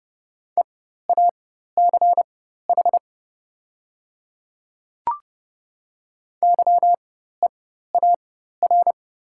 Morse code, heard as a keyed tone, about 30 wpm, spaced as for 14 wpm, 700 Hz. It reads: EACH YEAR
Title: 4x Character Spacing - Top 200 Words - 30wpm